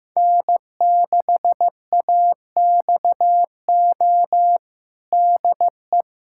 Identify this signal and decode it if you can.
Morse code, heard as a keyed tone, about 16 wpm, 700 Hz. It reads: N6AXO DE
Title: USA Callsign DE Callsign 15wpm